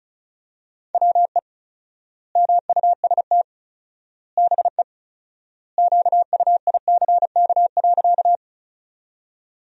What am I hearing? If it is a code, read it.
Morse code, heard as a keyed tone, about 36 wpm, 700 Hz. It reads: WE MUST BE QUICK.
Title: 4x Word-Spacing - Sentences from Top 500 Words - 35wpm